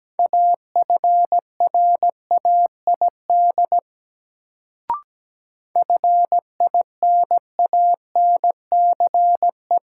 Morse code, heard as a keyed tone, about 17 wpm, 700 Hz. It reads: AFRAID FINANCE
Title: Sentences from Top 1500 Words - New Word Drill - 17wpm